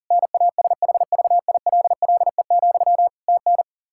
Morse code, heard as a keyed tone, about 40 wpm, 700 Hz. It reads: NASHVILLE, TN
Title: Practice QSO - Exchange 1 - Beginning - 40wpm